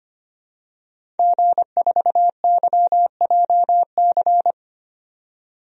Morse code, heard as a keyed tone, about 25 wpm, 700 Hz. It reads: G4YJC
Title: DXCC-100 Call Signs - 5 Characters - 25wpm